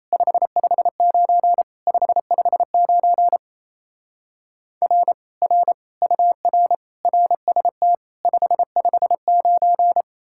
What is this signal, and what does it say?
Morse code, heard as a keyed tone, about 36 wpm, 700 Hz. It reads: 559 559 R R UR RST 559